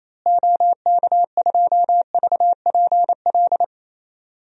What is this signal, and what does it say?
Morse code, heard as a keyed tone, about 28 wpm, 700 Hz. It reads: OK2VPL
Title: DXCC-100 Call Signs - 6 Characters - Repeated 2x - 28wpm